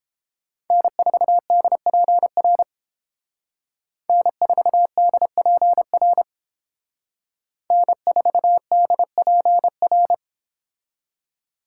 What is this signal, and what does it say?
Morse code, heard as a keyed tone, about 28 wpm, 700 Hz. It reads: N4DPR N4DPR N4DPR